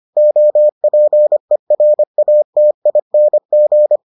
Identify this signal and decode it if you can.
Morse code, heard as a keyed tone, about 25 wpm, 600 Hz. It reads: OPERATING